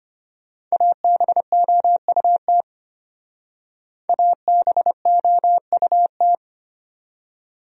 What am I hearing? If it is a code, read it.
Morse code, heard as a keyed tone, about 32 wpm, 700 Hz. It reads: ABOUT ABOUT